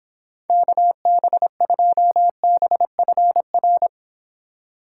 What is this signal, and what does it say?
Morse code, heard as a keyed tone, about 26 wpm, 700 Hz. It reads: KB2BFR